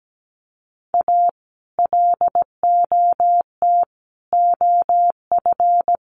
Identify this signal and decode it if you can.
Morse code, heard as a keyed tone, about 17 wpm, 700 Hz. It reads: A LOT OF
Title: Sets of 5 Words Encore 17wpm